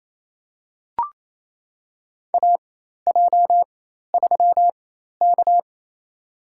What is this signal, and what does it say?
Morse code, heard as a keyed tone, about 28 wpm, spaced as for 13 wpm, 700 Hz. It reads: AJ3K
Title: USA Call Signs - 4 Characters - 4x Character Spacing - 28wpm